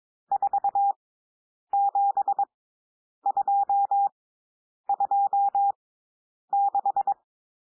Morse code, heard as a keyed tone, about 22 wpm, spaced as for 9 wpm, 800 Hz. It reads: 47226
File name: cwmp3 / N1 SOUND NUMBERS 22 wpm